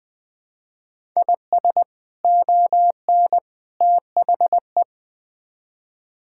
Morse code, heard as a keyed tone, about 20 wpm, 700 Hz. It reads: IS ON THE
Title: Build A Phrase - 3-Word Phrases - 20wpm